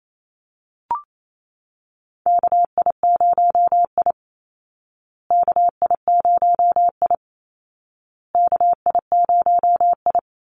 Morse code, heard as a keyed tone, about 28 wpm, 700 Hz. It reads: KI0I KI0I KI0I
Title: Call Signs Repeated 2x - 28wpm